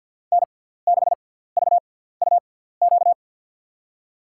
Morse code, heard as a keyed tone, about 52 wpm, spaced as for 17 wpm, 700 Hz. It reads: N6VUQ
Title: USA Call Signs - 5 Characters - 6x Character Spacing - 50wpm